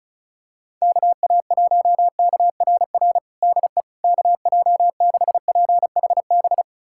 Morse code, heard as a keyed tone, about 34 wpm, 700 Hz. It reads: KA1KRR DE KJ6PHB
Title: USA Callsign DE Callsign - 6 Characters - 35wpm